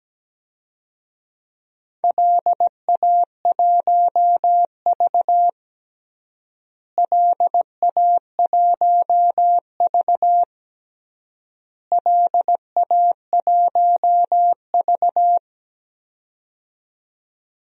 Morse code, heard as a keyed tone, about 17 wpm, 700 Hz. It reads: LA1V LA1V LA1V
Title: DXCC-100 Call Signs - 4 Characters - Repeated 3x - 17wpm